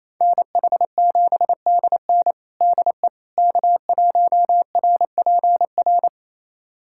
Morse code, heard as a keyed tone, about 28 wpm, 700 Hz. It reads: NH7DN DE K1RPR
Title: USA Callsign DE Callsign 28wpm